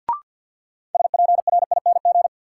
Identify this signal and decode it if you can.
Morse code, heard as a keyed tone, about 50 wpm, 700 Hz. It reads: SPRING
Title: Sentences from Top 1200 Words - New Word Drill Alt - 50wpm